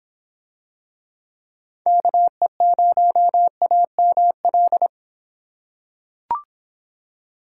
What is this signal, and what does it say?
Morse code, heard as a keyed tone, about 25 wpm, 700 Hz. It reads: KE0AML